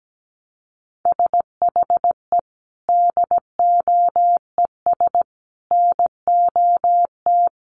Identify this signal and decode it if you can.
Morse code, heard as a keyed tone, about 17 wpm, 700 Hz. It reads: SHE DOES NOT
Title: Sets of 5 Words Encore 17wpm